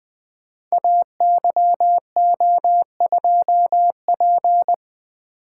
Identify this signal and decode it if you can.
Morse code, heard as a keyed tone, about 20 wpm, 700 Hz. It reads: AYO2P
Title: Code Groups - 5 Letter:Numbers - 20wpm